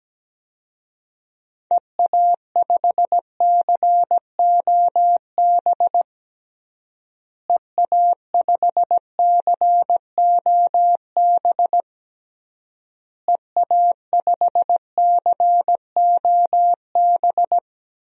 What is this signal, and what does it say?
Morse code, heard as a keyed tone, about 17 wpm, 700 Hz. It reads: EA5COB EA5COB EA5COB
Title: DXCC-100 Call Signs - 6 Characters - Repeated 3x - 17wpm